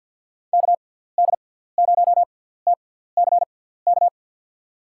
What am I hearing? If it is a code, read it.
Morse code, heard as a keyed tone, about 51 wpm, spaced as for 17 wpm, 700 Hz. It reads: KD0TCK
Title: USA Call Signs - 6 Characters - 6x Character Spacing - 50wpm